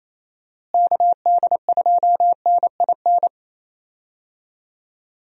Morse code, heard as a keyed tone, about 28 wpm, 700 Hz. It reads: KD2NIN
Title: USA Call Signs - 6 Characters - Repeated 3x - 28wpm